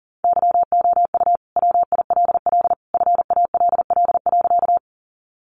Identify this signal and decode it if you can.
Morse code, heard as a keyed tone, about 40 wpm, 700 Hz. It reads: YOU WILL FALL.
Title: Sentences from Top 500 Words 40wpm